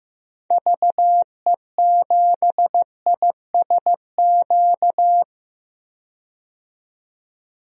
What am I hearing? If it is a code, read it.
Morse code, heard as a keyed tone, about 15 wpm, 700 Hz. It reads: VE7ISQ
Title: DXCC-100 Call Signs - 6 Characters - 15wpm